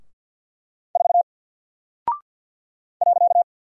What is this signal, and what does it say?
Morse code, heard as a keyed tone, about 48 wpm, 700 Hz. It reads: 4 .
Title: Single Character (Letter:Number:Punctuation:Prosign) Alt 50wpm